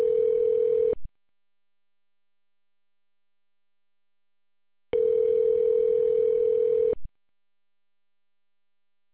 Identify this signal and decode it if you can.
Ringback tone (North America), 440 + 480 Hz.